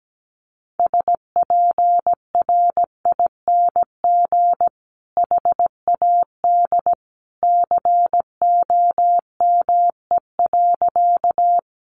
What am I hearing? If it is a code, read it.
Morse code, heard as a keyed tone, about 17 wpm, 700 Hz. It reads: SPRING HAD COME.
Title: Sentences from Top 1200 Words 17wpm